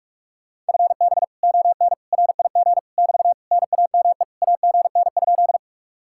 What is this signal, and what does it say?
Morse code, heard as a keyed tone, about 56 wpm, 700 Hz. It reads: FB ON RIG <BT> NAME AGN?
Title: Speed-Racing Common QSO Phrases - 45wpm